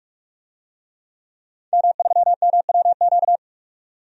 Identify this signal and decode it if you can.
Morse code, heard as a keyed tone, about 48 wpm, 700 Hz. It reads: M3MWQ
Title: DXCC-100 Call Signs - 5 Characters - 45wpm